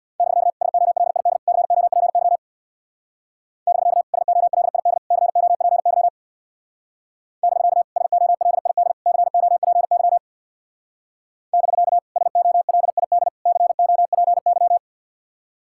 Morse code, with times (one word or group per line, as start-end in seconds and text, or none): 0.2-0.5 <BK>
0.6-1.4 SOLID
1.5-2.4 COPY
3.7-4.0 <BK>
4.1-5.0 SOLID
5.1-6.1 COPY
7.4-7.8 <BK>
8.0-8.9 SOLID
9.1-10.2 COPY
11.5-12.0 <BK>
12.2-13.3 SOLID
13.5-14.8 COPY